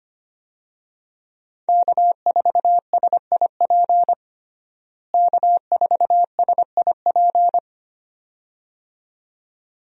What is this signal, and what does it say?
Morse code, heard as a keyed tone, about 25 wpm, 700 Hz. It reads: K4SIP K4SIP
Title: USA Call Signs - 5 Characters - Repeated 2x - 25wpm